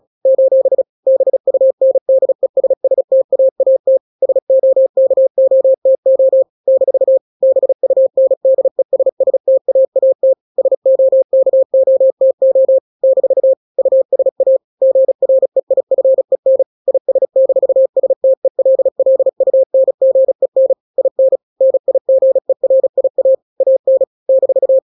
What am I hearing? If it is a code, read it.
Morse code, heard as a keyed tone, about 36 wpm, 530 Hz. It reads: : BUNDESSTAAT SOKOTO - BUNDESSTAAT SOKOTO - USA GREIFEN IS-STELLUNGEN IN NIGERIA AN -
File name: daily / 35wpm